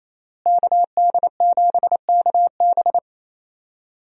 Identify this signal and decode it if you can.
Morse code, heard as a keyed tone, about 28 wpm, 700 Hz. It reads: KD7KB
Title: USA Call Signs - 5 Characters - Repeated 2x - 28wpm